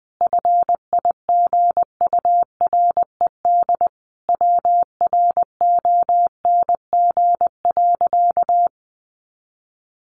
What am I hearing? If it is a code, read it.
Morse code, heard as a keyed tone, about 20 wpm, 700 Hz. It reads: FIGURED WRONG.
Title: A Sound Of Thunder 20wpm